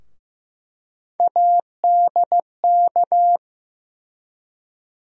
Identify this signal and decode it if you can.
Morse code, heard as a keyed tone, about 15 wpm, 700 Hz. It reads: ADK
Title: USA Call Sign Postfixes - 3 Letters 15wpm